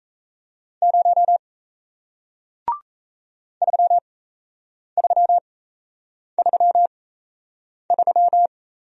Morse code, heard as a keyed tone, about 38 wpm, spaced as for 8 wpm, 700 Hz. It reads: 0 3333